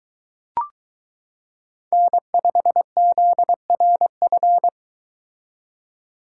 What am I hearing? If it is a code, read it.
Morse code, heard as a keyed tone, about 22 wpm, 700 Hz. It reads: N5ZRF